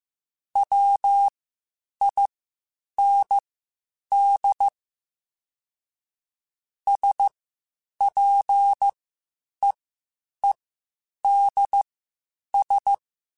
Morse code, heard as a keyed tone, about 15 wpm, spaced as for 8 wpm, 800 Hz. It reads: WIND SPEEDS
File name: Spaced Characters / Spaced Characters-mp3s / Weather 4 14 sp @13